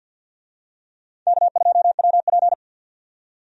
Morse code, heard as a keyed tone, about 46 wpm, 700 Hz. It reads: K2WP
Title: USA Call Signs - 4 Characters - 50wpm